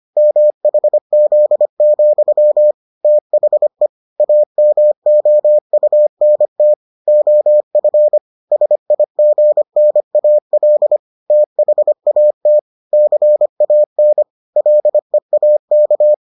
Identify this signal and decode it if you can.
Morse code, heard as a keyed tone, about 25 wpm, 600 Hz. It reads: MHZ, THE AMOUNT OF SIGNAL THAT CAN LEAK